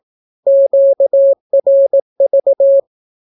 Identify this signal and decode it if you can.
Morse code, heard as a keyed tone, about 17 wpm, 550 Hz. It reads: QRV
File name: naukapodstaw / skroty / skroty